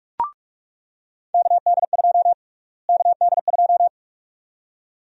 Morse code, heard as a keyed tone, about 46 wpm, 700 Hz. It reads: KD2 KD2
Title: USA Call Sign Prefixes - Repeated 2x - 45wpm